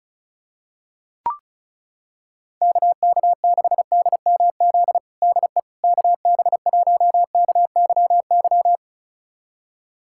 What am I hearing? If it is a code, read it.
Morse code, heard as a keyed tone, about 34 wpm, 700 Hz. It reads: KK6DMZ DE KB1KYY
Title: USA Callsign DE Callsign - 6 Characters - 35wpm